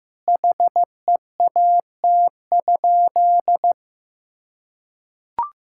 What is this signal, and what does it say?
Morse code, heard as a keyed tone, about 16 wpm, 700 Hz. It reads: HEAT?
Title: Sentences from Top 1000 Words - Review Part 6 - 15wpm